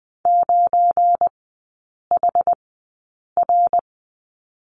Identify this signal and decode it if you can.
Morse code, heard as a keyed tone, about 20 wpm, spaced as for 8 wpm, 700 Hz. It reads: 9HR
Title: 172 Lesson 28e - Number 9 - Three Characters 20wpm